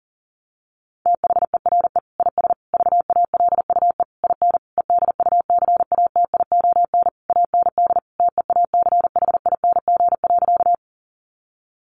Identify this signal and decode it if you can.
Morse code, heard as a keyed tone, about 40 wpm, 700 Hz. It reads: THERE IS VALUE IN EDUCATION AND TEACHING.
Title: Sentences from Top 1100 Words 40wpm